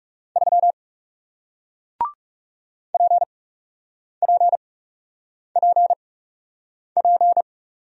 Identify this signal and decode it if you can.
Morse code, heard as a keyed tone, about 41 wpm, spaced as for 8 wpm, 700 Hz. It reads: 3 PPPP